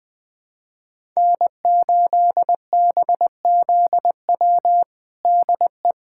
Morse code, heard as a keyed tone, about 20 wpm, 700 Hz. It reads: N8BZW DE
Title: USA Callsign DE Callsign 20wpm